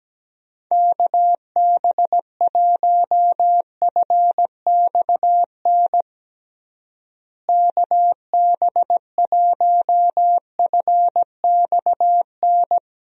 Morse code, heard as a keyed tone, about 17 wpm, 700 Hz. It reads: KB1FXN KB1FXN
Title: USA Call Signs - 6 Characters - Repeated 3x - 17wpm